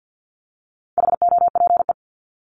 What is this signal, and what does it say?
Morse code, heard as a keyed tone, about 48 wpm, 700 Hz. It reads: HOPE
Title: Top 751-1000 Words 50wpm